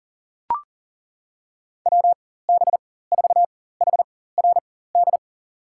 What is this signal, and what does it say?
Morse code, heard as a keyed tone, about 40 wpm, spaced as for 19 wpm, 700 Hz. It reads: WB4HRD
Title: USA Call Signs - 6 Characters - 4x Character Spacing - 40wpm